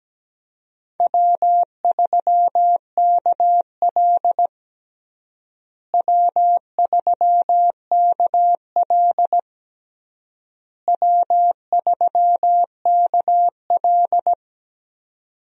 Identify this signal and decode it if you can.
Morse code, heard as a keyed tone, about 17 wpm, 700 Hz. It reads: W3KL W3KL W3KL
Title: USA Call Signs - 4 Characters - Repeated 3x - 17wpm